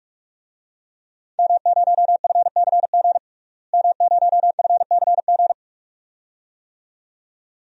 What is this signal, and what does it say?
Morse code, heard as a keyed tone, about 41 wpm, 700 Hz. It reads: M0FCG M0FCG
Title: DXCC-100 Call Signs - 5 Characters - Repeated 2x - 45wpm